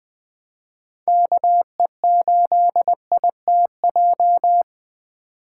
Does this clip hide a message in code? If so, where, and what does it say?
Yes: Morse code from 1.1 to 4.6 s, heard as a keyed tone, about 20 wpm, 700 Hz. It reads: KE8ITJ